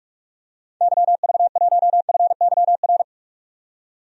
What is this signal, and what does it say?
Morse code, heard as a keyed tone, about 41 wpm, 700 Hz. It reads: YV1FYR
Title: DXCC-100 Call Signs - 6 Characters - 45wpm